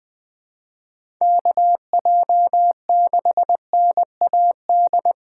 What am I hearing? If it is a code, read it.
Morse code, heard as a keyed tone, about 20 wpm, 700 Hz. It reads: KJ6NAD
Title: USA Call Signs - 6 Characters - 4x Character Spacing - 20wpm